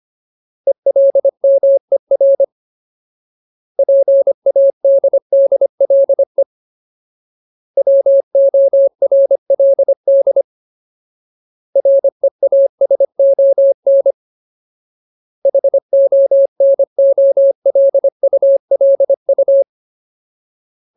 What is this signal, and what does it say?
Morse code, heard as a keyed tone, about 25 wpm, 550 Hz. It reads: ELMER PADDLE WORLD REASON HONOLULU
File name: Ultimate Word List / 25WPM Ultimate